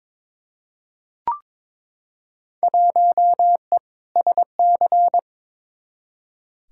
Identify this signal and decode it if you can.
Morse code, heard as a keyed tone, about 22 wpm, 700 Hz. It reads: 1E SC
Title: Field Day Exchanges - 22wpm